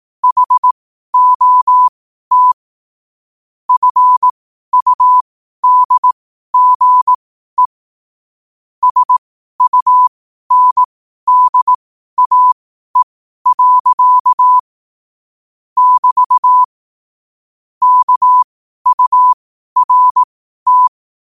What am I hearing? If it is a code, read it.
Morse code, heard as a keyed tone, about 18 wpm, spaced as for 12 wpm, 1000 Hz. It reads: HOT FUDGE SUNDAE. <BT> KURT
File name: 12wpm